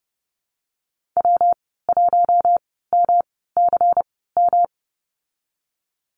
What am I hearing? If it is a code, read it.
Morse code, heard as a keyed tone, about 30 wpm, spaced as for 17 wpm, 700 Hz. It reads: W1MCM
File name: Lesson 19g - Letter w - Call Signs 30wpm